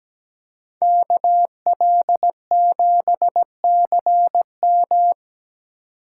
Morse code, heard as a keyed tone, about 20 wpm, 700 Hz. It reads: KL7CM
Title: Speed Racing - USA 2x2 Call Signs 17wpm